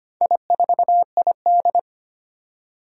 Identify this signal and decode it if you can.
Morse code, heard as a keyed tone, about 25 wpm, 700 Hz. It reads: I4ID
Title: Four Challenging Letter-Number Combinations 25wpm